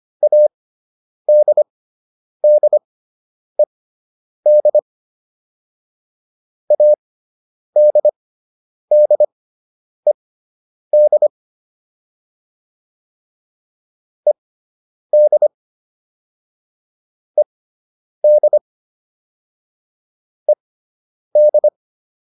Morse code, heard as a keyed tone, about 25 wpm, spaced as for 9 wpm, 600 Hz. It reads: ADDED ADDED ED ED ED
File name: ED5 09